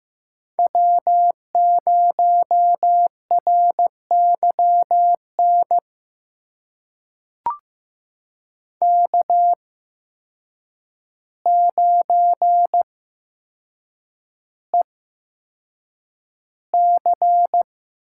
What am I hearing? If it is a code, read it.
Morse code, heard as a keyed tone, about 15 wpm, spaced as for 5 wpm, 700 Hz. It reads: W0RY NK9EC
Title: USA Call Signs - 8x Character Spacing - 15wpm